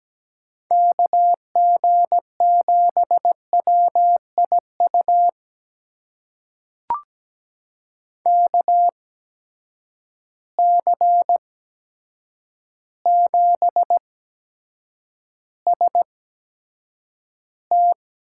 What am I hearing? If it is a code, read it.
Morse code, heard as a keyed tone, about 17 wpm, spaced as for 8 wpm, 700 Hz. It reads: KG7WIU KC7ST
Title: USA Call Signs - 6 Characters - 8x Character Spacing - 17wpm